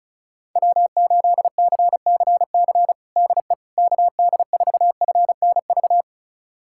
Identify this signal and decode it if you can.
Morse code, heard as a keyed tone, about 34 wpm, 700 Hz. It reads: W8CCC DE KD4FNV
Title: USA Callsign DE Callsign 35wpm